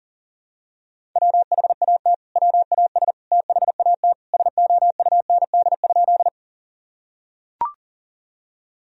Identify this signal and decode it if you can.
Morse code, heard as a keyed tone, about 40 wpm, 700 Hz. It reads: WHAT WAS THAT SOUND?
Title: Sentences from Top 1000 Words - Review Part 1 - 40wpm